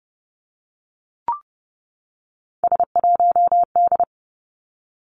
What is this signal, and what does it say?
Morse code, heard as a keyed tone, about 30 wpm, 700 Hz. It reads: S1D
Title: Three Challenging Letter-Number Combinations 30wpm